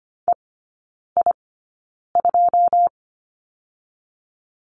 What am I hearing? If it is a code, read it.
Morse code, heard as a keyed tone, about 25 wpm, spaced as for 9 wpm, 690 Hz. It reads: EI2
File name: Lesson 14e - Number 2 - Three Characters 25wpm